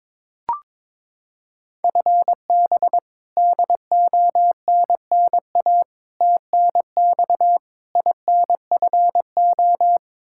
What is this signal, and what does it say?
Morse code, heard as a keyed tone, about 22 wpm, 700 Hz. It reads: FB DONNA TNX INFO
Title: Practice QSO - Exchange 3 - More Info - 22wpm